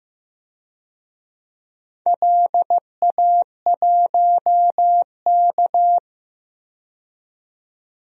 Morse code, heard as a keyed tone, about 15 wpm, 700 Hz. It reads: LA1K
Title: DXCC-100 Call Signs - 4 Characters - 15wpm